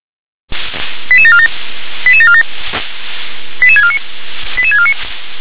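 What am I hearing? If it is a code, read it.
Five-tone selective call (ZVEI), heard as quick runs of five tones, tones of about 80 ms. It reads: 88747 88747 88649 88649
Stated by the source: ZVEI five-tone selective call received by SDR.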